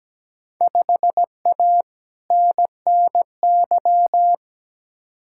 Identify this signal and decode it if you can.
Morse code, heard as a keyed tone, about 17 wpm, 700 Hz. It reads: 5A NNY
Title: Field Day Exchanges - 17wpm